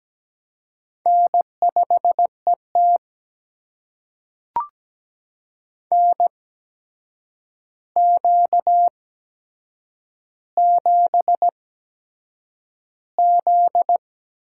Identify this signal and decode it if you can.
Morse code, heard as a keyed tone, about 17 wpm, spaced as for 7 wpm, 700 Hz. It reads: N5ET NQ7Z
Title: USA Call Signs - 4 Characters - 8x Character Spacing - 17wpm